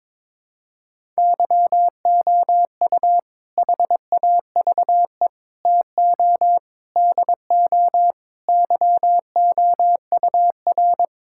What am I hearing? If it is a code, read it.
Morse code, heard as a keyed tone, about 22 wpm, 700 Hz. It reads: YOU HAVE TO DO YOUR
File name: practice / 22wpm / Sentences with 4-Word Phrases - 22wpm